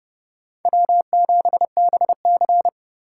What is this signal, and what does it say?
Morse code, heard as a keyed tone, about 30 wpm, 700 Hz. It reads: W7BC
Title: USA Call Signs - 4 Characters - 2x Character Spacing - 30wpm